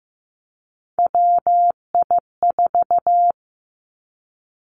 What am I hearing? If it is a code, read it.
Morse code, heard as a keyed tone, about 15 wpm, 700 Hz. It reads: WI4
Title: Three Challenging Letter-Number Combinations 15wpm